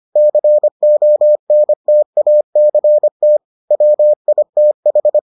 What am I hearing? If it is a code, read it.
Morse code, heard as a keyed tone, about 25 wpm, 600 Hz. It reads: CONTACT WITH